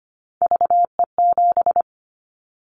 Morse code, heard as a keyed tone, about 25 wpm, 700 Hz. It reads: VE7
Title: Callsign Prefixes for Top 25 DXCC - 25wpm